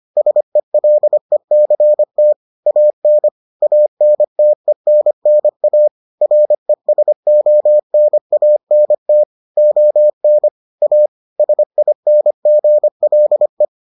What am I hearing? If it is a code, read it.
Morse code, heard as a keyed tone, about 25 wpm, 600 Hz. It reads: SELECT AN ANTENNA RESONANT ON A SINGLE